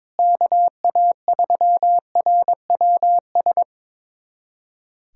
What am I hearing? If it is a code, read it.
Morse code, heard as a keyed tone, about 22 wpm, 700 Hz. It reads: KA3RWS
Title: USA Callsign DE Callsign - 6 Characters - 22wpm